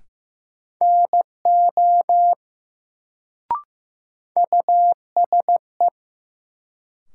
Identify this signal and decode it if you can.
Morse code, heard as a keyed tone, about 15 wpm, 700 Hz. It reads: NO USE
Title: Sentences from Top 100 Words - New Word Drill - 15wpm